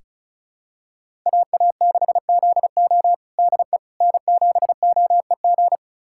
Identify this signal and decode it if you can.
Morse code, heard as a keyed tone, about 34 wpm, 700 Hz. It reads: AA6ZO DE N7OEG
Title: USA Callsign DE Callsign - 5 Characters - 35wpm